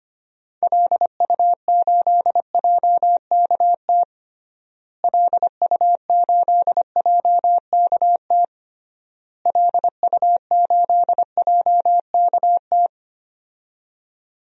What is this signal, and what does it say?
Morse code, heard as a keyed tone, about 25 wpm, 700 Hz. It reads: LU8JKT LU8JKT LU8JKT
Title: DXCC-100 Call Signs - 6 Characters - Repeated 3x - 25wpm